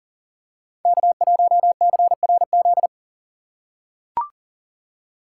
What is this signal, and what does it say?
Morse code, heard as a keyed tone, about 40 wpm, 700 Hz. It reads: K1CRZ
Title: USA Call Signs - 5 Characters - 2x Character Spacing - 40wpm